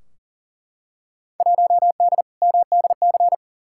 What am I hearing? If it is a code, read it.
Morse code, heard as a keyed tone, about 39 wpm, 700 Hz. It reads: 1D MDC
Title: Field Day Exchanges - 40wpm